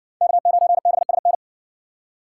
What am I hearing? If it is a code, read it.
Morse code, heard as a keyed tone, about 60 wpm, 700 Hz. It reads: K9BSN